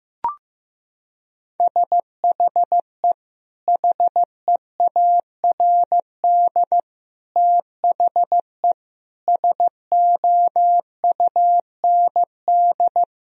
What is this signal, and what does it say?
Morse code, heard as a keyed tone, about 15 wpm, 700 Hz. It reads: SHE HEARD THE SOUND
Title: Sentences from Top 1000 Words - Review Part 1 - 15wpm